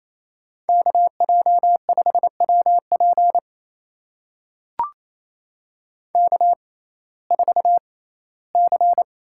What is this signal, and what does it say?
Morse code, heard as a keyed tone, about 28 wpm, spaced as for 17 wpm, 700 Hz. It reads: KJ5WP K4C
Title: USA Call Signs - 6x Character Spacing - 28wpm